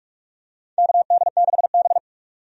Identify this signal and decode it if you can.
Morse code, heard as a keyed tone, about 46 wpm, 700 Hz. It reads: KD6B
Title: USA Call Signs - 4 Characters - 45wpm